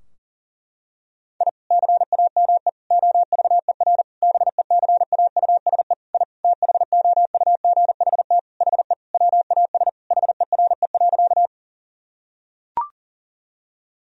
Morse code, heard as a keyed tone, about 40 wpm, 700 Hz. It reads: I CAME OVER BECAUSE I THOUGHT HE WAS HERE.
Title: Sentences from Top 1000 Words - Review Part 3 - 40wpm